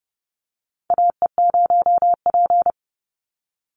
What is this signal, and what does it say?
Morse code, heard as a keyed tone, about 30 wpm, 700 Hz. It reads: AE0P
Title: Call Signs - 30wpm